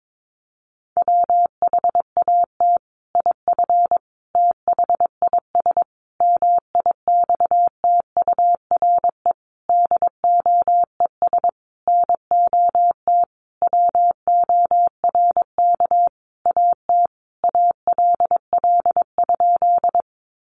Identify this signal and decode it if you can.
Morse code, heard as a keyed tone, about 22 wpm, 700 Hz. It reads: WHAT IF THIS MIXTURE DOES NOT WORK AT ALL?
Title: Sentences from Top 1500 Words 22wpm